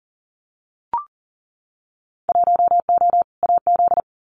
Morse code, heard as a keyed tone, about 38 wpm, 700 Hz. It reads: 1O AZ
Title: Winter Field Day Exchanges 40wpm